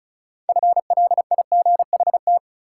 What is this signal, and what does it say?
Morse code, heard as a keyed tone, about 36 wpm, 700 Hz. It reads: FLIGHT
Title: Sentences from Top 1000 Words - New Word Drill - 35wpm